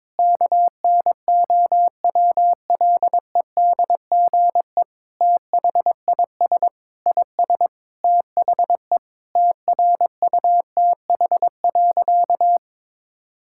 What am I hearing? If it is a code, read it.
Morse code, heard as a keyed tone, about 22 wpm, 700 Hz. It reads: KNOWLEDGE THIS IS THE TRUTH.
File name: Sentences with 9-Word Phrases - 22wpm